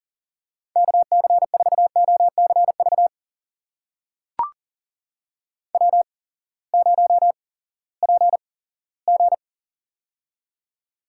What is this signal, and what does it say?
Morse code, heard as a keyed tone, about 40 wpm, spaced as for 20 wpm, 700 Hz. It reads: KC4OCV W0PG
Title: USA Call Signs - 8x Character Spacing - 40wpm